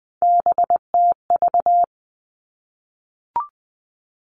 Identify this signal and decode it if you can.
Morse code, heard as a keyed tone, about 19 wpm, 700 Hz. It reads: BTV
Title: Three Challenging Letter-Number Combinations 20wpm